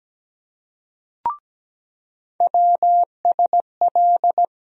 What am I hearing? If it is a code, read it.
Morse code, heard as a keyed tone, about 17 wpm, 700 Hz. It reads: WSL
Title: USA Call Sign Postfixes - 3 Letters 17wpm